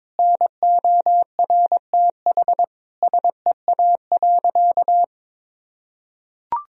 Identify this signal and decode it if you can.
Morse code, heard as a keyed tone, about 22 wpm, 700 Hz. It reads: NORTH SEA.
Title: Sentences from Top 1000 Words - Review Part 1 - 22wpm